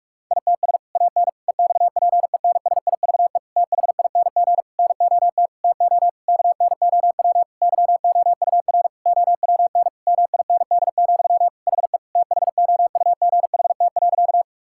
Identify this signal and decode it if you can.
Morse code, heard as a keyed tone, about 46 wpm, 700 Hz. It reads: ITS AN EXPENSIVE THING NOT TO KNOW YOUR OWN MIND, HE THOUGHT.
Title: Sentences from Top 1000 Words - Review Part 6 - 45wpm